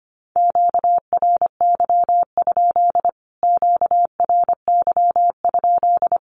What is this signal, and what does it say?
Morse code, heard as a keyed tone, about 25 wpm, 700 Hz. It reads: QRY? QRY?
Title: Q-Codes Repeated 3x 25wpm